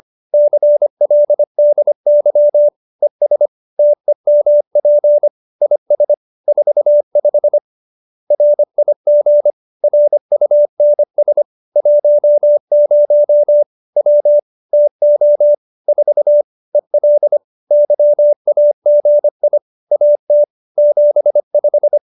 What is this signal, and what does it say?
Morse code, heard as a keyed tone, about 25 wpm, 590 Hz. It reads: CLDY ES TEMP IS 45 RIG RUNS 10 W TO 4 EL YAGI AT 75